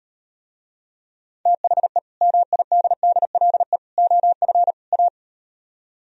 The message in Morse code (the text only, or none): THE MIDDLE OF A